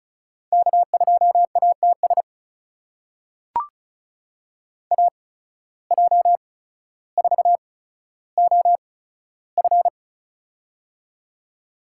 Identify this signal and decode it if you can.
Morse code, heard as a keyed tone, about 35 wpm, spaced as for 15 wpm, 690 Hz. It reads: K2ATS AJ4OF
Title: USA Call Signs - 5 Characters - 8x Character Spacing - 35wpm